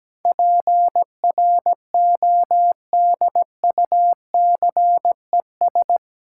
Morse code, heard as a keyed tone, about 17 wpm, 700 Hz. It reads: PRODUCES
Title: Sentences from Top 1000 Words - Review Part 6 - New Word Drill - 17wpm